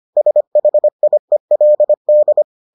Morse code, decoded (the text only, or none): SHIELD